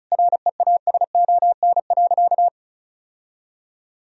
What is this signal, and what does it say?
Morse code, heard as a keyed tone, about 36 wpm, 700 Hz. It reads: REASON.
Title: Sentences from Top 1000 Words - Review Part 1 - 35wpm